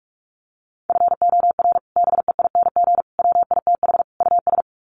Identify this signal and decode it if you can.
Morse code, heard as a keyed tone, about 44 wpm, 700 Hz. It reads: FOR BEING WITH US
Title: Sets of 4 Words 45wpm